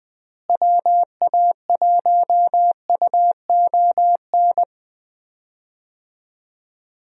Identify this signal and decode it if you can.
Morse code, heard as a keyed tone, about 20 wpm, 700 Hz. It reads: WA1UON
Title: USA Call Signs - 6 Characters - Repeated 2x - 20wpm